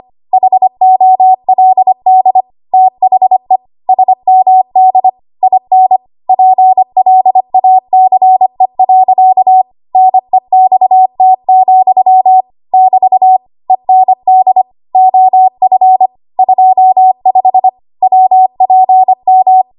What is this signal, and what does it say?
Morse code, heard as a keyed tone, about 25 wpm, 750 Hz. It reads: HOLD THE SMD IN PLACE. NEXT, <BT> END OF 25 WPM
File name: Archive / 25 WPM / 25WPM